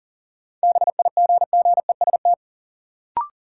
Morse code, heard as a keyed tone, about 39 wpm, 700 Hz. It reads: BIGGEST
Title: Sets of 4 Words - New Word Drill - 40wpm